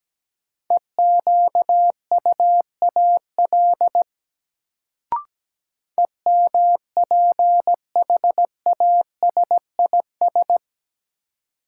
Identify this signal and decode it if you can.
Morse code, heard as a keyed tone, about 17 wpm, 700 Hz. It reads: EQUAL EMPHASIS
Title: Sentences from Top 1400 Words - New Word Drill - 17wpm